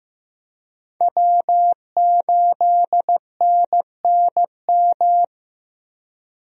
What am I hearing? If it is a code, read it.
Morse code, heard as a keyed tone, about 15 wpm, 700 Hz. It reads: W8NNM
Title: USA Call Signs - 5 Characters - 15wpm